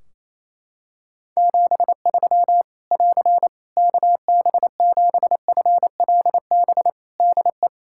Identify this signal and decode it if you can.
Morse code, heard as a keyed tone, about 28 wpm, 700 Hz. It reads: 73 <AR> KB7FLB DE
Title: Practice QSO - Exchange 4 - Ending - 28wpm